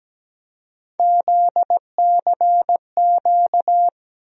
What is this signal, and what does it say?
Morse code, heard as a keyed tone, about 17 wpm, 700 Hz. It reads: ZCQ
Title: USA Call Sign Postfixes - 3 Letters 17wpm